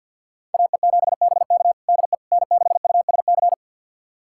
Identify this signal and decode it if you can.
Morse code, heard as a keyed tone, about 48 wpm, 700 Hz. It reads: AE7BK DE N6USC
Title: USA Callsign DE Callsign - 5 Characters - 50wpm